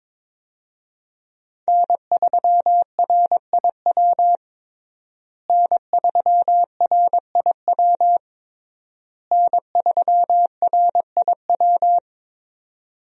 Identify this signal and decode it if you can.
Morse code, heard as a keyed tone, about 22 wpm, 700 Hz. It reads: N3RIW N3RIW N3RIW
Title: USA Call Signs - 5 Characters - Repeated 3x - 22wpm